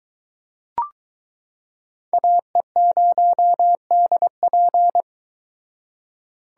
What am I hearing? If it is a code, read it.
Morse code, heard as a keyed tone, about 22 wpm, 700 Hz. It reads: AE0DP